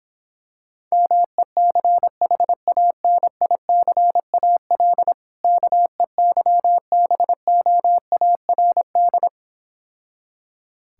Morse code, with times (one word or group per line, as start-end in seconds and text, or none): 0.9-5.1 MECHANICAL
5.4-9.3 KEYBOARD